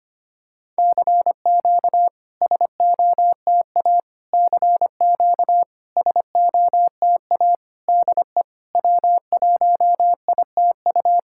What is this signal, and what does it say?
Morse code, heard as a keyed tone, about 25 wpm, 700 Hz. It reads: CQ SOTA CQ SOTA DE W1ITU
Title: SOTA QSO Practice 25wpm